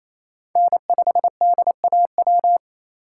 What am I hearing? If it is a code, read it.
Morse code, heard as a keyed tone, about 28 wpm, 700 Hz. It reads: N5DAW
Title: USA Call Signs - 5 Characters - 4x Character Spacing - 28wpm